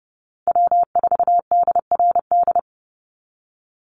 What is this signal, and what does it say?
Morse code, heard as a keyed tone, about 30 wpm, 700 Hz. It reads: W4DRD
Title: USA Call Signs 30wpm